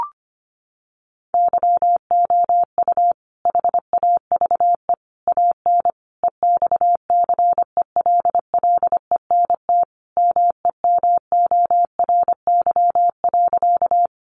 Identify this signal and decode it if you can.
Morse code, heard as a keyed tone, about 25 wpm, 700 Hz. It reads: YOU HAVE AN EXCELLENT MEMORY.
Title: Sentences from Top 1300 Words 25wpm